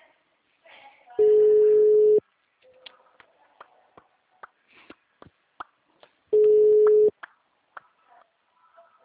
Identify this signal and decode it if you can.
Ringback tone (Europe, 425 Hz), with speech.
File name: out- -2003- - - .36